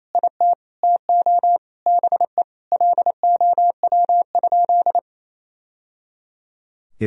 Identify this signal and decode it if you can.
Morse code, heard as a keyed tone, about 28 wpm, 700 Hz. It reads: IT TO BE LOW?
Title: Sentences from Top 1000 Words - Review Part 1 - 28wpm